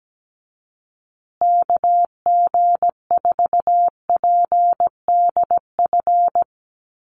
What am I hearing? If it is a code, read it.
Morse code, heard as a keyed tone, about 17 wpm, 700 Hz. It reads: KG4PDF
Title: USA Call Signs Repeated 3x 17wpm